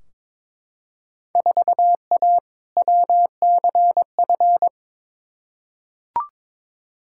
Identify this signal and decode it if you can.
Morse code, heard as a keyed tone, about 22 wpm, 700 Hz. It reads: 4A WCF
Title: Field Day Exchanges - 22wpm